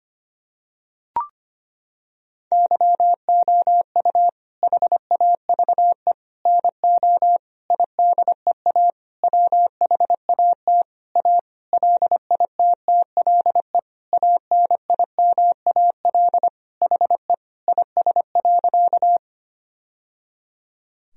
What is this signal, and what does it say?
Morse code, heard as a keyed tone, about 25 wpm, 700 Hz. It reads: YOU HAVE NO IDEA WHAT A LITTLE ANIMAL HE IS.
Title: Sentences from Top 1000 Words - Review Part 1 - 25wpm